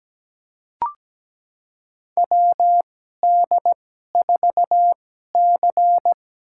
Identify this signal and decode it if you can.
Morse code, heard as a keyed tone, about 17 wpm, spaced as for 12 wpm, 700 Hz. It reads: WD4C
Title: USA Call Signs - 2x Character Spacing - 17wpm